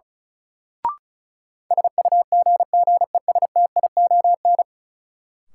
Morse code, heard as a keyed tone, about 36 wpm, 700 Hz. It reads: SUGGESTION
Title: Sentences from Top 1600 Words - New Word Drill Alt - 35wpm